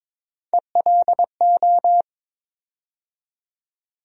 Morse code, heard as a keyed tone, about 22 wpm, 700 Hz. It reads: ELO
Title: USA Call Sign Postfixes - 3 Letters 22wpm